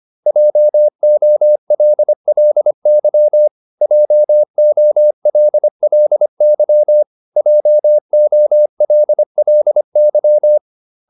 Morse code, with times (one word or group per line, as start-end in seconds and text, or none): 0.3-3.5 JOLLY
3.8-7.0 JOLLY
7.4-10.6 JOLLY